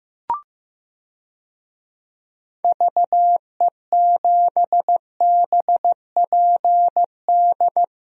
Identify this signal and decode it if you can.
Morse code, heard as a keyed tone, about 16 wpm, 700 Hz. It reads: VE7BPD
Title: DXCC-100 Call Signs - 6 Characters - Repeated 3x - 15wpm